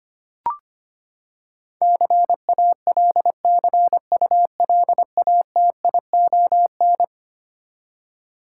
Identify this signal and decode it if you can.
Morse code, heard as a keyed tone, about 25 wpm, 700 Hz. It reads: CALCULATION
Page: morse-code-ninja-practice-25wpm